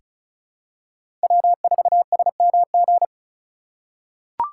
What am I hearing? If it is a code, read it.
Morse code, heard as a keyed tone, about 34 wpm, 700 Hz. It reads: W4SMG
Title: USA Call Signs - 5 Characters - 2x Character Spacing - 35wpm